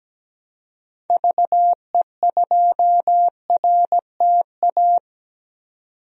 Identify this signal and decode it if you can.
Morse code, heard as a keyed tone, about 17 wpm, 700 Hz. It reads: VE2RTA
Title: DXCC-100 Call Signs - 6 Characters - 17wpm